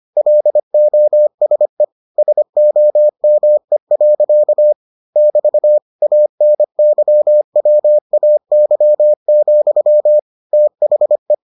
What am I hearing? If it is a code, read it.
Morse code, heard as a keyed tone, about 25 wpm, 600 Hz. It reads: LOSE SOME. <BT> ANYWAY, THE